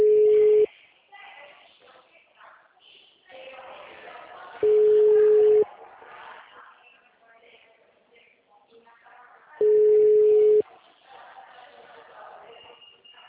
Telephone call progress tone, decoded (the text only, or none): ringback tone